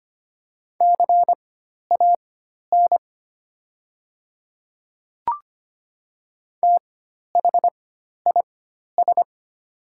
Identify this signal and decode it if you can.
Morse code, heard as a keyed tone, about 25 wpm, spaced as for 12 wpm, 700 Hz. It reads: CAN THIS
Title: 4x Character Spacing - Top 500 Words - 25wpm